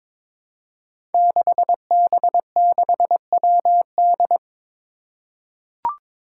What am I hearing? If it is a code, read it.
Morse code, heard as a keyed tone, about 22 wpm, 700 Hz. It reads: 6B6WD
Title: Five Challenging Letter-Number Combinations 22wpm